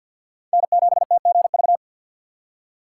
Morse code, heard as a keyed tone, about 53 wpm, 690 Hz. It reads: N7TGV
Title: USA Call Signs - 5 Characters - 4x Character Spacing - 50wpm